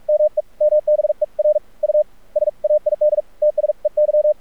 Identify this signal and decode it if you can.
Morse code, heard as a keyed tone, about 46 wpm, spaced as for 39 wpm, 600 Hz. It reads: MEMBER U SAID TSEY
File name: archived / CRSnet